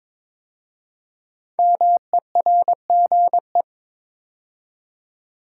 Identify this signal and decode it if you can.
Morse code, heard as a keyed tone, about 22 wpm, 690 Hz. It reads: MERGE
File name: 5 Letter Words - First 3 Letters Given - 22wpm